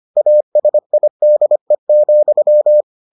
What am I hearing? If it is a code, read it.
Morse code, heard as a keyed tone, about 25 wpm, 600 Hz. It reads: ASIDE,